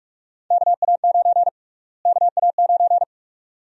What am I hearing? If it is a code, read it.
Morse code, heard as a keyed tone, about 46 wpm, 700 Hz. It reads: KA9 KA9
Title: USA Call Sign Prefixes - Repeated 2x - 45wpm